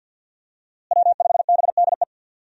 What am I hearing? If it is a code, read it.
Morse code, heard as a keyed tone, about 52 wpm, 700 Hz. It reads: W5BDE
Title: USA Call Signs - 5 Characters - Repeated 2x - 50wpm